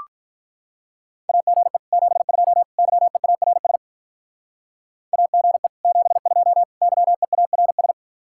Morse code, heard as a keyed tone, about 44 wpm, 700 Hz. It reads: AGE 72 YEARS AGE 72 YEARS